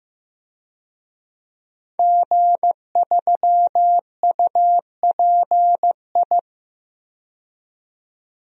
Morse code, heard as a keyed tone, about 15 wpm, 700 Hz. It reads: G3UPI
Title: DXCC-100 Call Signs - 5 Characters - 15wpm